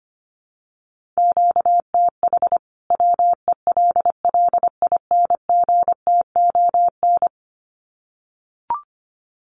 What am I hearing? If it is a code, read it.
Morse code, heard as a keyed tone, about 25 wpm, 700 Hz. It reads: QTH WELLINGTON
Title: Common QSO Phrases 25wpm